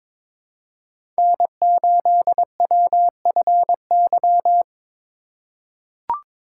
Morse code, heard as a keyed tone, about 22 wpm, 700 Hz. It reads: N8WFY
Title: USA Call Signs - 5 Characters - 3x Character Spacing - 22wpm